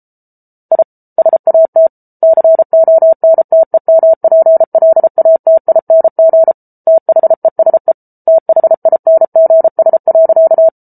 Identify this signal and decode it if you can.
Morse code, heard as a keyed tone, about 34 wpm, 650 Hz. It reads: I SAT CONTEMPLATING THESE THINGS.